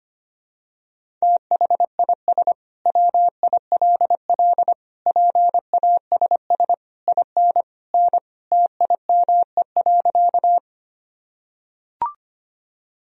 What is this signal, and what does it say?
Morse code, heard as a keyed tone, about 25 wpm, 700 Hz. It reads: THIS WILL PASS IN N TIME.
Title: Sentences from Top 400 Words and Abbreviations 25wpm